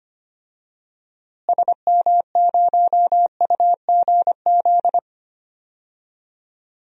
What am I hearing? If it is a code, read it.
Morse code, heard as a keyed tone, about 25 wpm, 700 Hz. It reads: SM0UGZ
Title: DXCC-100 Call Signs - 6 Characters - Repeated 3x - 25wpm